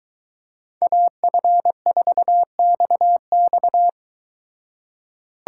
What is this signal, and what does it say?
Morse code, heard as a keyed tone, about 23 wpm, 700 Hz. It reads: AF4XX